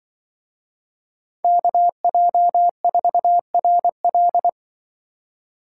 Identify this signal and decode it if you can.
Morse code, heard as a keyed tone, about 24 wpm, 700 Hz. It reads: KJ4RL
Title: Speed Racing - USA 2x2 Call Signs 20wpm